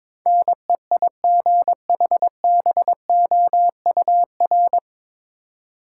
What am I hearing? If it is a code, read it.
Morse code, heard as a keyed tone, about 22 wpm, 700 Hz. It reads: NEIGHBOUR
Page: Morse Code Ninja practice 22wpm